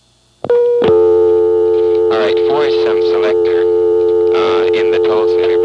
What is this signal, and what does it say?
Dial tone (North America), 350 + 440 Hz, with speech.
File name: MoreheadCityNCNoTest